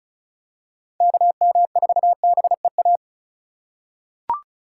Morse code, heard as a keyed tone, about 36 wpm, 700 Hz. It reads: KM4BEA
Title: USA Call Signs - 2x Character Spacing - 35wpm